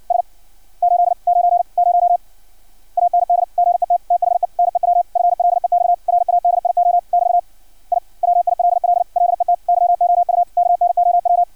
Morse code, heard as a keyed tone, about 56 wpm, 700 Hz. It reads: S 890 AND GET THE NEW FLEX RADIO <BT> I WILL LET YOU KNOW
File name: archived / CRSnet